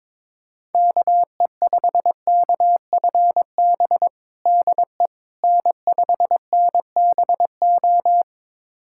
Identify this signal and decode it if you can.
Morse code, heard as a keyed tone, about 22 wpm, 700 Hz. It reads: KE5KFB DE N5NBO
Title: USA Callsign DE Callsign 22wpm